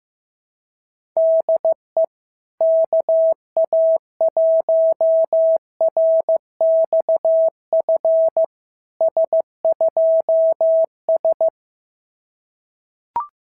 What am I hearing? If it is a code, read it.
Morse code, heard as a keyed tone, about 14 wpm, 690 Hz. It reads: DE KA1RXF S2S
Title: SOTA QSO Practice 15wpm